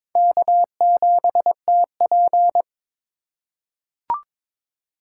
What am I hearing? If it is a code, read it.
Morse code, heard as a keyed tone, about 22 wpm, 700 Hz. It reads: K7TP
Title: USA Call Signs - 4 Characters - 3x Character Spacing - 22wpm